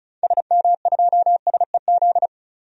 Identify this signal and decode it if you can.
Morse code, heard as a keyed tone, about 36 wpm, 700 Hz. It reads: SM2SEZ
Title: DXCC-100 Call Signs - 6 Characters - 35wpm